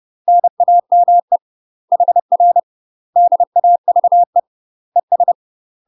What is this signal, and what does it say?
Morse code, heard as a keyed tone, about 30 wpm, 700 Hz. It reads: NAME HR DAVE ES
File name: UK MASTER QSOs TRAINING / 30WPM / UK MASTER QSOs 0040